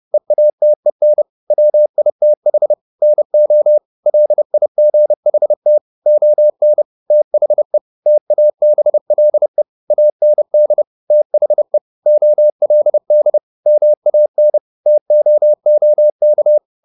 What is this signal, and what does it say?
Morse code, heard as a keyed tone, about 30 wpm, 600 Hz. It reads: EATEN WITH NO LIGHT ON THE TABLE AND THE OLD MAN TOOK
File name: the old man and the sea / the old man and the sea 1 30wpm